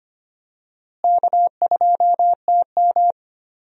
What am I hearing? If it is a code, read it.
Morse code, heard as a keyed tone, about 25 wpm, 700 Hz. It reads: K2TM
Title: USA Call Signs - 4 Characters - Repeated 2x - 25wpm